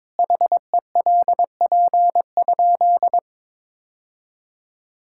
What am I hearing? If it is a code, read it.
Morse code, heard as a keyed tone, about 22 wpm, 700 Hz. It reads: HELP?
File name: practice / 22wpm / Sentences with 4-Word Phrases - 22wpm